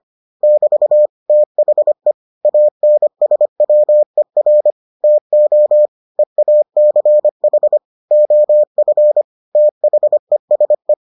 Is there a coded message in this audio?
Yes: Morse code from 0.4 to 10.9 s, heard as a keyed tone, about 25 wpm, 600 Hz. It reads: <BT> THE ANSWER TO EACH OF THESE